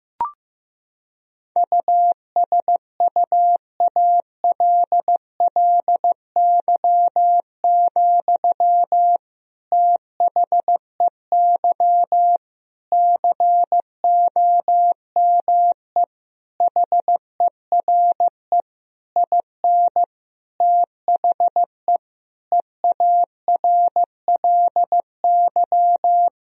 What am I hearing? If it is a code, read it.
Morse code, heard as a keyed tone, about 16 wpm, 700 Hz. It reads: USUALLY, THEY COME HERE IN THE EARLY
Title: Sentences from Top 1000 Words - Review Part 3 - 15wpm